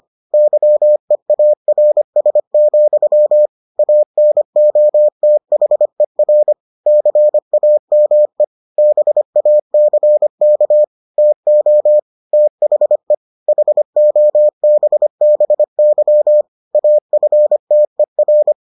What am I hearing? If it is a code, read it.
Morse code, heard as a keyed tone, about 25 wpm, 600 Hz. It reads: YEARS, ANOTHER CAME BACK TO THE HOBBY AFTER